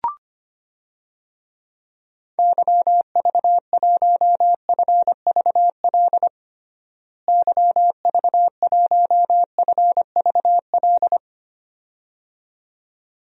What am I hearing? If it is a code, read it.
Morse code, heard as a keyed tone, about 25 wpm, 700 Hz. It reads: YV1FVL YV1FVL
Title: DXCC-100 Call Signs - 6 Characters - Repeated 2x - 25wpm